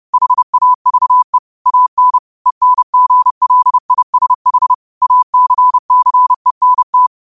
Morse code, heard as a keyed tone, about 30 wpm, 1000 Hz. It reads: HAVE AN ENGLISH ACCENT